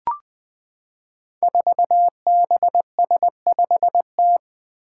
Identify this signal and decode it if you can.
Morse code, heard as a keyed tone, about 21 wpm, 700 Hz. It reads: 4BS5T
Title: Five Challenging Letter-Number Combinations 20wpm